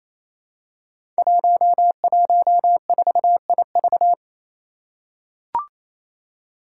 Morse code, heard as a keyed tone, about 29 wpm, 700 Hz. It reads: 114IV
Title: Five Challenging Letter-Number Combinations 28wpm